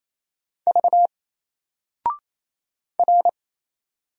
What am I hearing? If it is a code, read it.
Morse code, heard as a keyed tone, about 27 wpm, 700 Hz. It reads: V R
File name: Single-Letter-Number-28wpm